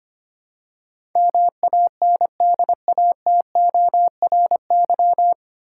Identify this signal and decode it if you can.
Morse code, heard as a keyed tone, about 25 wpm, 700 Hz. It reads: MANDATORY
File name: Nine Letter Words - 25wpm